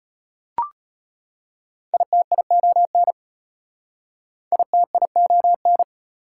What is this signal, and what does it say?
Morse code, heard as a keyed tone, about 34 wpm, 700 Hz. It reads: ITION ITION